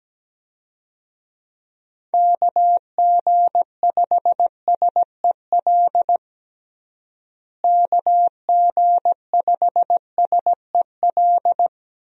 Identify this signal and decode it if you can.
Morse code, heard as a keyed tone, about 17 wpm, 700 Hz. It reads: KG5SEL KG5SEL
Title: USA Call Signs - 6 Characters - Repeated 3x - 17wpm